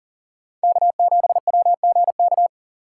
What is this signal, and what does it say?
Morse code, heard as a keyed tone, about 40 wpm, 700 Hz. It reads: K7WGK
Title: USA Call Signs - 5 Characters - 4x Character Spacing - 40wpm